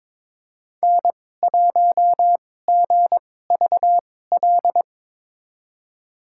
Morse code, heard as a keyed tone, about 22 wpm, spaced as for 16 wpm, 700 Hz. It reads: N1GVL
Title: USA Call Signs - 5 Characters - 2x Character Spacing - 22wpm